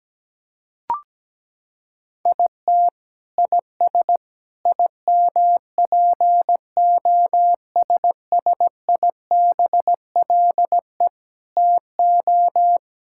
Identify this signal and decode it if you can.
Morse code, heard as a keyed tone, about 17 wpm, 700 Hz. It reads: IT IS IMPOSSIBLE TO
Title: Sentences from Top 1600 Words 17wpm